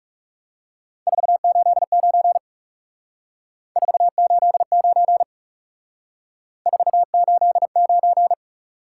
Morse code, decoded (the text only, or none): 489 489 489